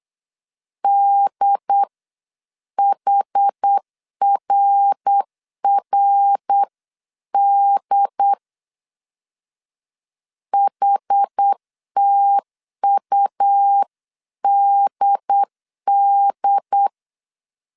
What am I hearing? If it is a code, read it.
Morse code, heard as a keyed tone, about 8 wpm, 800 Hz. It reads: DHRRD HTUDD